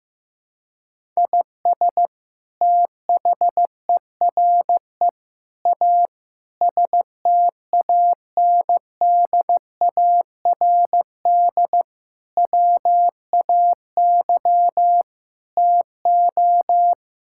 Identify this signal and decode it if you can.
Morse code, heard as a keyed tone, about 15 wpm, 700 Hz. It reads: IS THERE A STANDARD WAY TO
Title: Sentences from Top 1000 Words - Review Part 6 - 15wpm